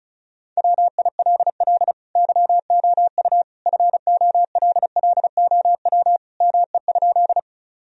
Morse code, heard as a keyed tone, about 36 wpm, 700 Hz. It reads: WILL YOU FOLLOW ME?
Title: Sentences from Top 1000 Words - Review Part 1 - 35wpm